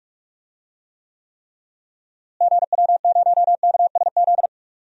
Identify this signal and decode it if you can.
Morse code, heard as a keyed tone, about 41 wpm, 700 Hz. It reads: GW0KSZ
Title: DXCC-100 Call Signs - 6 Characters - Repeated 3x - 45wpm